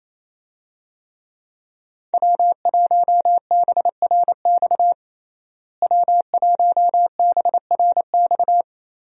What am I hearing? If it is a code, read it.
Morse code, heard as a keyed tone, about 28 wpm, 700 Hz. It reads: W1BRX W1BRX
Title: USA Call Signs - 5 Characters - Repeated 3x - 28wpm